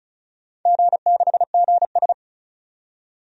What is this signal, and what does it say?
Morse code, heard as a keyed tone, about 34 wpm, 700 Hz. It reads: G6GS
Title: DXCC-100 Call Signs - 4 Characters - 35wpm